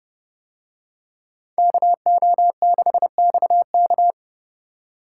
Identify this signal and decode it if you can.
Morse code, heard as a keyed tone, about 29 wpm, 700 Hz. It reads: KO6XK